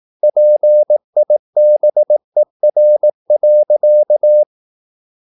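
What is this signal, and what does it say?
Morse code, heard as a keyed tone, about 18 wpm, 600 Hz. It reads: PIBER.
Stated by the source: Slow (QRS) Morse practice at 18 wpm, CW med Gnister.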